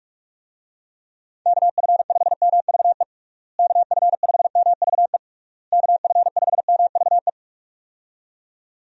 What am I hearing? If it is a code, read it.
Morse code, heard as a keyed tone, about 46 wpm, 700 Hz. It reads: KF5MVE KF5MVE KF5MVE
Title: USA Call Signs - 6 Characters - Repeated 3x - 45wpm